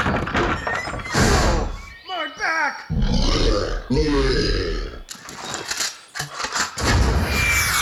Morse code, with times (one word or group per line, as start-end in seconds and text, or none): none